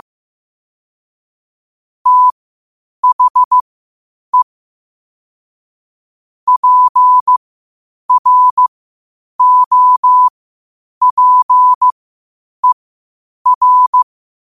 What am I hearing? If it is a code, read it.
Morse code, heard as a keyed tone, about 15 wpm, spaced as for 8 wpm, 1000 Hz. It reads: THE PROPER
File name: qotdfiles / 8wpm